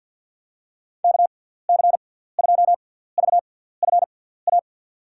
Morse code, heard as a keyed tone, about 54 wpm, spaced as for 17 wpm, 700 Hz. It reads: KC2VFA
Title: USA Call Signs - 6 Characters - 6x Character Spacing - 50wpm